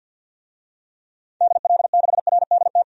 Morse code, heard as a keyed tone, about 48 wpm, 700 Hz. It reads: DL6RDT
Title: DXCC-100 Call Signs - 6 Characters - 50wpm